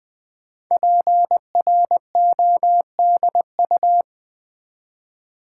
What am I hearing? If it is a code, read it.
Morse code, heard as a keyed tone, about 20 wpm, 700 Hz. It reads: PRODU
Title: Build A Word - 5 Letters At A Time - 12-Letter Words - 20wpm